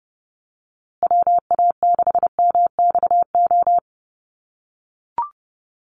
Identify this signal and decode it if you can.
Morse code, heard as a keyed tone, about 30 wpm, 700 Hz. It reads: WA6MXO
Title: USA Call Signs Repeated 2x 30wpm